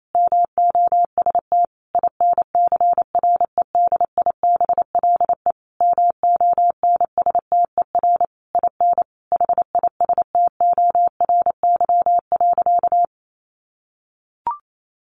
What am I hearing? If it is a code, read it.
Morse code, heard as a keyed tone, about 28 wpm, 700 Hz. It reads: MOST INCREDIBLE MONSTER IN HISTORY.
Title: A Sound Of Thunder 28wpm